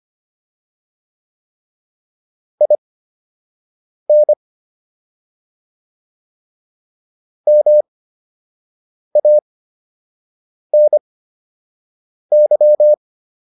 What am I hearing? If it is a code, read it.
Morse code, heard as a keyed tone, about 25 wpm, spaced as for 6 wpm, 600 Hz. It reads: IN MANY